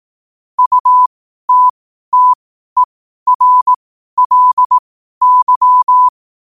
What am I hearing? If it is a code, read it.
Morse code, heard as a keyed tone, about 18 wpm, spaced as for 12 wpm, 1000 Hz. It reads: UTTERLY
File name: qotdfiles / 12wpm